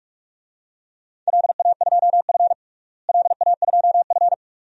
Morse code, heard as a keyed tone, about 41 wpm, 700 Hz. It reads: LA2F LA2F
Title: DXCC-100 Call Signs - 4 Characters - Repeated 2x - 45wpm